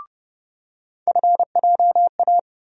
Morse code, heard as a keyed tone, about 30 wpm, 700 Hz. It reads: FJA
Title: USA Call Sign Postfixes - 3 Letters 30wpm